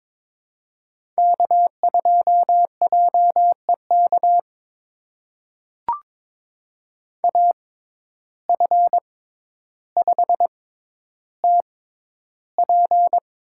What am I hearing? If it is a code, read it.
Morse code, heard as a keyed tone, about 22 wpm, spaced as for 11 wpm, 700 Hz. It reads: K2JEK AF5TP
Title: USA Call Signs - 5 Characters - 6x Character Spacing - 22wpm